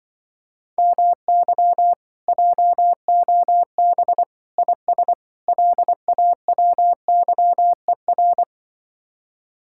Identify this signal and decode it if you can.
Morse code, heard as a keyed tone, about 23 wpm, 700 Hz. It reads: MY JOB IS LAWYER
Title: Speed-Racing Common QSO Phrases - 20wpm